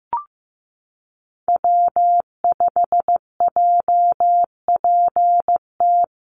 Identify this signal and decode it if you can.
Morse code, heard as a keyed tone, about 15 wpm, 700 Hz. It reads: W5JPT
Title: USA Call Signs Repeated 2x 15wpm